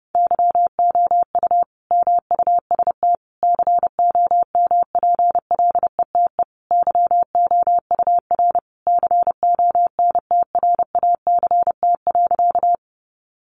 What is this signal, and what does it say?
Morse code, heard as a keyed tone, about 30 wpm, 700 Hz. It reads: YOU MUST COMPLETE YOUR CONTRACT.
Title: Sentences from Top 1400 Words 30wpm